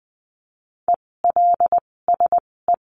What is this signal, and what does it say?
Morse code, heard as a keyed tone, about 20 wpm, spaced as for 16 wpm, 700 Hz. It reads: ELSE
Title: 256 Lesson 42a - Words 12-20wpm